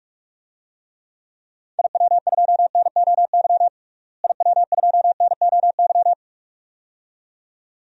Morse code, heard as a keyed tone, about 46 wpm, 700 Hz. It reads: IW2NOY IW2NOY
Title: DXCC-100 Call Signs - 6 Characters - Repeated 2x - 45wpm